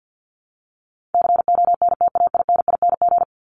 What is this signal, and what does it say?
Morse code, heard as a keyed tone, about 49 wpm, 700 Hz. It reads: CONTAINING
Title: ING Suffix - Top 100 Words with Base Words 50wpm